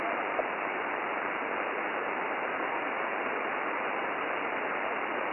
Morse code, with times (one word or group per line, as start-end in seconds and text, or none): none